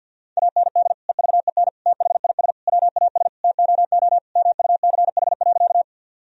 Morse code, heard as a keyed tone, about 51 wpm, 700 Hz. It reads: AND EVEN THIS WAS TOO MUCH.
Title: Sentences from Top 1000 Words - Review Part 1 - 50wpm